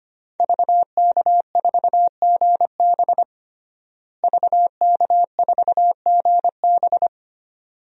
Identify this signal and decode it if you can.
Morse code, heard as a keyed tone, about 25 wpm, 700 Hz. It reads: VK4GB VK4GB
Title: DXCC-100 Call Signs - 5 Characters - Repeated 2x - 25wpm